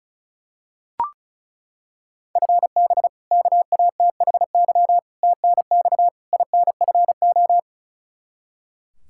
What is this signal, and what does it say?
Morse code, heard as a keyed tone, about 34 wpm, 700 Hz. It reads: FB KATHY TNX INFO
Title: Practice QSO - Exchange 3 - More Info - 35wpm